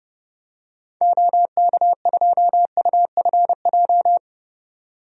Morse code, heard as a keyed tone, about 30 wpm, 700 Hz. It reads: OK2UFJ
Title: DXCC-100 Call Signs - 6 Characters - 30wpm